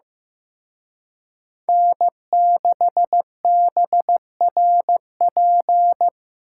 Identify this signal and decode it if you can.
Morse code, heard as a keyed tone, about 18 wpm, 700 Hz. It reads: N6BRP
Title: Speed Racing - USA 1x3 Call Signs 15wpm